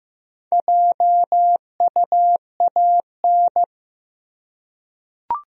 Morse code, heard as a keyed tone, about 15 wpm, 700 Hz. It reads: JUAN
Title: USA Common Names - Max 4 Letters - 15wpm